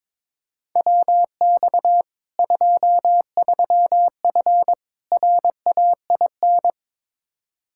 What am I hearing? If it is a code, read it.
Morse code, heard as a keyed tone, about 22 wpm, 700 Hz. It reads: WX 23F RAIN
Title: Practice QSO - Exchange 2 - Info - 22wpm